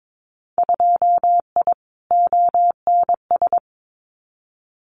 Morse code, heard as a keyed tone, about 22 wpm, 700 Hz. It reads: 2I ONS
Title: Winter Field Day Exchanges Encore 22wpm